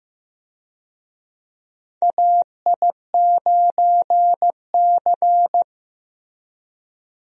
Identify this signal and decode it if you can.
Morse code, heard as a keyed tone, about 15 wpm, 700 Hz. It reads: AI9C
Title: USA Call Signs - 4 Characters - Repeated 3x - 15wpm